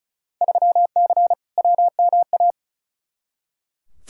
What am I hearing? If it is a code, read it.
Morse code, heard as a keyed tone, about 36 wpm, 700 Hz. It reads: 3C WMA
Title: Field Day Exchanges Encore - 35wpm